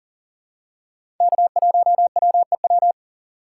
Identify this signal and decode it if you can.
Morse code, heard as a keyed tone, about 39 wpm, 700 Hz. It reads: K1WEW
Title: USA Call Signs - 4x Character Spacing - 40wpm